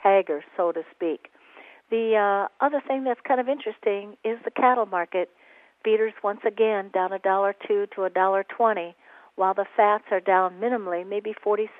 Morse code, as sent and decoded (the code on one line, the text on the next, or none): none